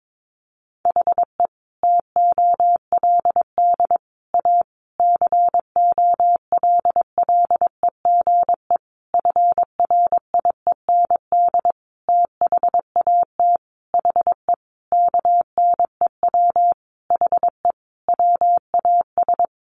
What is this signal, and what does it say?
Morse code, heard as a keyed tone, about 22 wpm, 700 Hz. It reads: HE TOLD A COLLEGE FRIEND THAT HE KNEW HE WAS
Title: Sentences from Top 1300 Words 22wpm